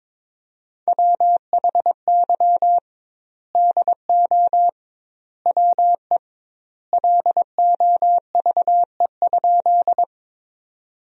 Morse code, heard as a keyed tone, about 22 wpm, 700 Hz. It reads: WHY DO WE LOVE?
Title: 2x Word-Spacing - Sentences from Top 400 Words - 22wpm